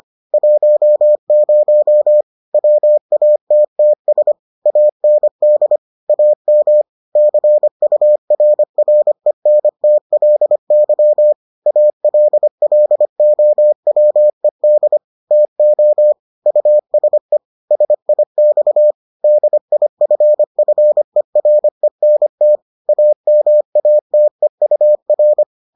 Morse code, heard as a keyed tone, about 25 wpm, 600 Hz. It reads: 10 WATTS AND AM CURRENTLY ALLOWED TO USE SIX DIFFERENT AMATEUR